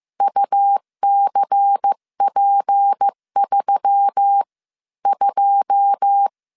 Morse code, heard as a keyed tone, about 15 wpm, spaced as for 12 wpm, 800 Hz. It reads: UCP32
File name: CW / DET / 1 9 1500C S B 14wpm 3302